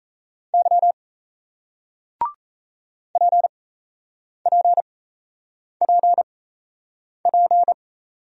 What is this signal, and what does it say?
Morse code, heard as a keyed tone, about 37 wpm, spaced as for 8 wpm, 700 Hz. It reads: Y PPPP